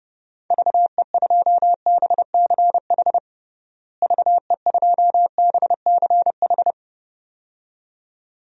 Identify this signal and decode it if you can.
Morse code, heard as a keyed tone, about 32 wpm, 700 Hz. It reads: VE2BCH VE2BCH
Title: DXCC-100 Call Signs - 6 Characters - Repeated 2x - 30wpm